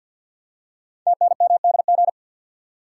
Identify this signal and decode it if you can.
Morse code, heard as a keyed tone, about 48 wpm, 700 Hz. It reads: TNMDG
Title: Code Groups - 5 Letters - 50wpm